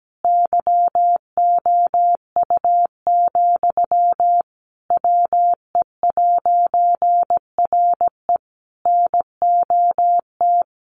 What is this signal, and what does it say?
Morse code, heard as a keyed tone, about 17 wpm, 700 Hz. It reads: YOU, WE'RE NOT
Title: A Sound Of Thunder 17wpm